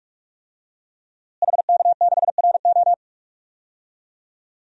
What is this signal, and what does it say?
Morse code, heard as a keyed tone, about 42 wpm, 700 Hz. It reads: HK6RO
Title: DXCC-100 Call Signs - 5 Characters - Repeated 3x - 45wpm